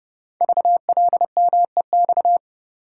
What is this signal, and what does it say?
Morse code, heard as a keyed tone, about 30 wpm, 700 Hz. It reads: VLMEX
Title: Code Groups - 5 Letters - 30wpm